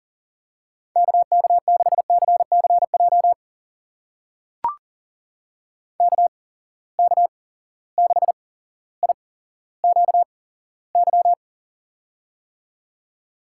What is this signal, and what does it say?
Morse code, heard as a keyed tone, about 40 wpm, spaced as for 17 wpm, 700 Hz. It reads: KK6CCJ KK6IQY
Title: USA Call Signs - 6 Characters - 8x Character Spacing - 40wpm